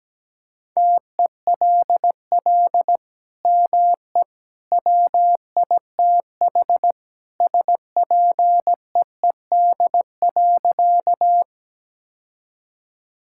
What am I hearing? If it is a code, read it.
Morse code, heard as a keyed tone, about 17 wpm, 700 Hz. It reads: TELL ME WITH SPEED.
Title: Sentences from Top 1000 Words - Review Part 5 - 17wpm